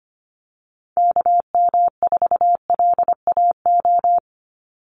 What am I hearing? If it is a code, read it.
Morse code, heard as a keyed tone, about 25 wpm, 700 Hz. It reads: KM4LAO
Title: Call Signs Repeated 3x - 25wpm